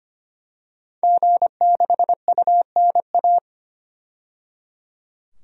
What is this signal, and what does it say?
Morse code, heard as a keyed tone, about 25 wpm, 700 Hz. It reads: G6UNA
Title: DXCC-100 Call Signs - 5 Characters - 25wpm